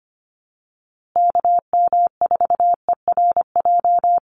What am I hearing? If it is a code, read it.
Morse code, heard as a keyed tone, about 25 wpm, 700 Hz. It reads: KM4ERJ
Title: USA Call Signs Repeated 2x 25wpm